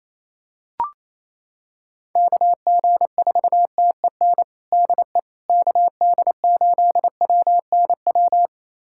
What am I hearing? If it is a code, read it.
Morse code, heard as a keyed tone, about 27 wpm, 700 Hz. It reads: KG4TEN DE KD8WNW
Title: USA Callsign DE Callsign - 6 Characters - 28wpm